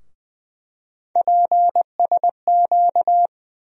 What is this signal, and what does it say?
Morse code, heard as a keyed tone, about 21 wpm, 700 Hz. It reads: PSQ
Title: USA Call Sign Postfixes - 3 Letters 20wpm